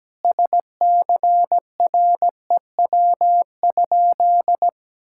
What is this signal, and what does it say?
Morse code, heard as a keyed tone, about 17 wpm, 700 Hz. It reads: SCREW?
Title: Sentences from Top 1600 Words 17wpm